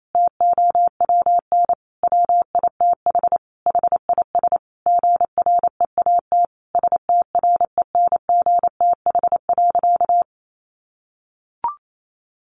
Morse code, heard as a keyed tone, about 28 wpm, 700 Hz. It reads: TOWN WITH HIS GREAT STRENGTH.
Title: Sentences from Top 1300 Words 28wpm